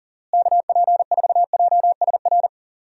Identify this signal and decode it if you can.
Morse code, heard as a keyed tone, about 40 wpm, 700 Hz. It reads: KP4JSR
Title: USA Call Signs - 6 Characters - Repeated 2x - 40wpm